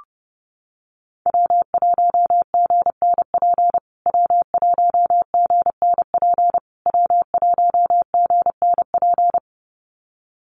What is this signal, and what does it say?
Morse code, heard as a keyed tone, about 29 wpm, 700 Hz. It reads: W1GNP W1GNP W1GNP
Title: USA Call Signs Repeated 3x 30wpm